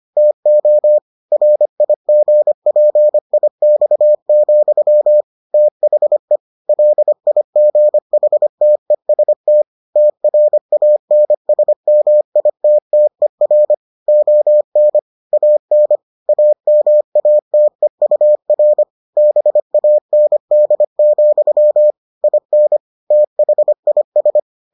Morse code, heard as a keyed tone, about 25 wpm, 600 Hz. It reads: TO RIGPIX, THE LIGHTEST TRANSMITTER ON AN AMATEUR BAND, IN THIS